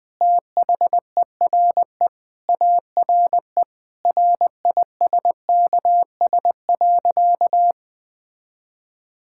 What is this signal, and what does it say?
Morse code, heard as a keyed tone, about 20 wpm, 700 Hz. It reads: THERE ARE RISKS.
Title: Sentences with 5-Word Phrases - 20wpm